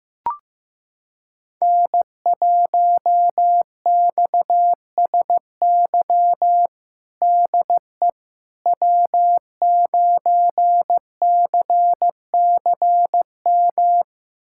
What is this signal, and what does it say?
Morse code, heard as a keyed tone, about 15 wpm, 700 Hz. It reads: N1XSY DE W9CCM
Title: USA Callsign DE Callsign - 5 Characters - 15wpm